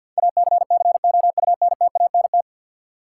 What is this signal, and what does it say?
Morse code, heard as a keyed tone, about 50 wpm, 700 Hz. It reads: ACCOUNTANT
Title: Practice QSO - Exchange 3 - More Info - 50wpm